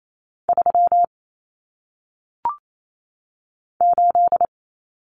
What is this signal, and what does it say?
Morse code, heard as a keyed tone, about 28 wpm, 700 Hz. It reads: 3 8
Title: Common Cut Numbers - 1-Digit 28wpm